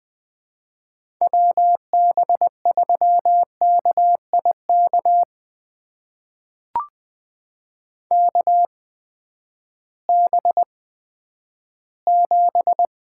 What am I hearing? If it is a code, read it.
Morse code, heard as a keyed tone, about 20 wpm, spaced as for 11 wpm, 700 Hz. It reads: WB3KIK KB7
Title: USA Call Signs - 6 Characters - 8x Character Spacing - 20wpm